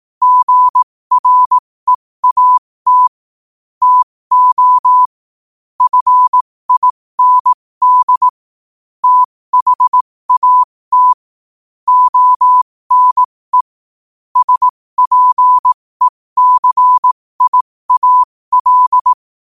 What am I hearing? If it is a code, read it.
Morse code, heard as a keyed tone, about 18 wpm, spaced as for 15 wpm, 1000 Hz. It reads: GREAT TO FIND THAT ONE SPECIAL
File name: 15wpm